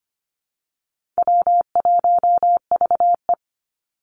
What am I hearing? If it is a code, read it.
Morse code, heard as a keyed tone, about 25 wpm, 700 Hz. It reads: W1VE
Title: Easy Call Signs - 25wpm